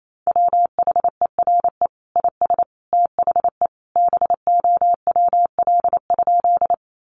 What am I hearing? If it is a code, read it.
Morse code, heard as a keyed tone, about 28 wpm, 700 Hz. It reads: WHERE IS THE BOWL?
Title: Sentences from Top 1200 Words 28wpm